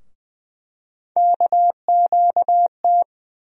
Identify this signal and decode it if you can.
Morse code, heard as a keyed tone, about 20 wpm, 700 Hz. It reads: KQT
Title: USA Call Sign Postfixes - 3 Letters 20wpm